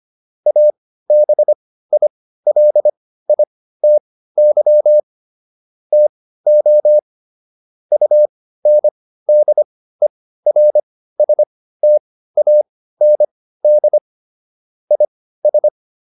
Morse code, heard as a keyed tone, about 25 wpm, spaced as for 15 wpm, 600 Hz. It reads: ABILITY TO UNDERSTAND IS